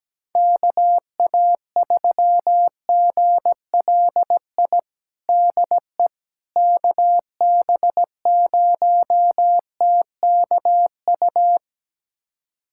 Morse code, heard as a keyed tone, about 17 wpm, 700 Hz. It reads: KA3GLI DE KB0TKU
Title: USA Callsign DE Callsign - 6 Characters - 17wpm